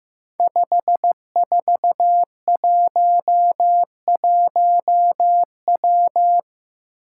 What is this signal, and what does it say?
Morse code, heard as a keyed tone, about 15 wpm, 700 Hz. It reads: 5411W
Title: Five Challenging Letter-Number Combinations 15wpm